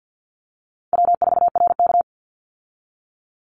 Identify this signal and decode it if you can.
Morse code, heard as a keyed tone, about 50 wpm, 700 Hz. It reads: W4RK
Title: Call Signs Repeated 2x - 50wpm